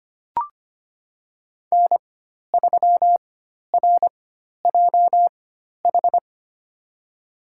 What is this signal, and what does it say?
Morse code, heard as a keyed tone, about 25 wpm, spaced as for 12 wpm, 700 Hz. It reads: N3RJH
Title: USA Call Signs - 5 Characters - 4x Character Spacing - 25wpm